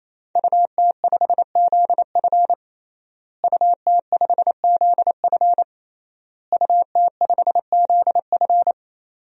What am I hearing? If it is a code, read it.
Morse code, heard as a keyed tone, about 27 wpm, 700 Hz. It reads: UT5ZF UT5ZF UT5ZF
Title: DXCC-100 Call Signs - 5 Characters - Repeated 3x - 28wpm